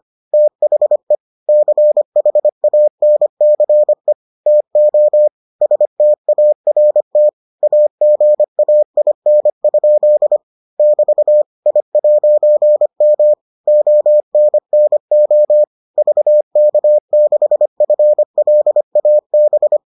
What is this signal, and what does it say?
Morse code, heard as a keyed tone, about 25 wpm, 600 Hz. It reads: THE CHANCE TO START AGAIN? <BT> I'M ONNO VK6FLAB